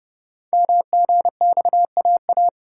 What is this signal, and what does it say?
Morse code, heard as a keyed tone, about 29 wpm, 700 Hz. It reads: MGXAA
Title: Code Groups - 5 Letters - 30wpm